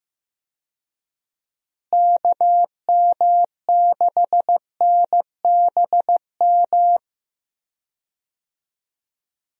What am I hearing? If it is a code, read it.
Morse code, heard as a keyed tone, about 15 wpm, 700 Hz. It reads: KM6NBM
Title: USA Call Signs - 6 Characters - Repeated 2x - 15wpm